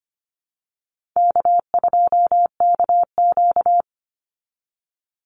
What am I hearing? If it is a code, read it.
Morse code, heard as a keyed tone, about 25 wpm, 700 Hz. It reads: K2KQ
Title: Easy Call Signs Repeated 2x - 25wpm